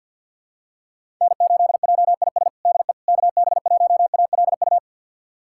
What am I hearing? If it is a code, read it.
Morse code, heard as a keyed tone, about 52 wpm, 700 Hz. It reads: N8JIS DE KB1ALU
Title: USA Callsign DE Callsign 50wpm